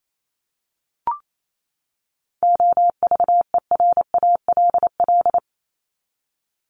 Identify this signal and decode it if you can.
Morse code, heard as a keyed tone, about 28 wpm, 700 Hz. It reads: OVERALL
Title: Top 751-1000 Words 28wpm